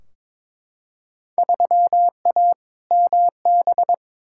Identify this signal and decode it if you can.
Morse code, heard as a keyed tone, about 22 wpm, 700 Hz. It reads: 3A MB
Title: Field Day Exchanges Encore - 22wpm